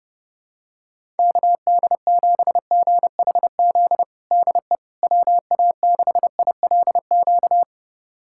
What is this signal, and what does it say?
Morse code, heard as a keyed tone, about 32 wpm, 700 Hz. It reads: KD7GHZ DE WA6ILQ
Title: USA Callsign DE Callsign - 6 Characters - 30wpm